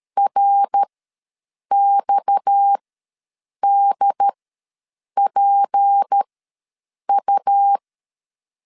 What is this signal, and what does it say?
Morse code, heard as a keyed tone, about 13 wpm, spaced as for 7 wpm, 800 Hz. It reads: RXDPU
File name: LEZIONI / 7 GF9Z / 7 GF ws 12wpm 3140